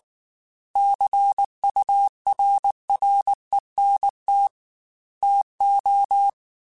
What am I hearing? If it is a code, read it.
Morse code, heard as a keyed tone, about 19 wpm, 800 Hz. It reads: CURRENT TO
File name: Technical Subjects / Technical Subjects-mp3s / XC @18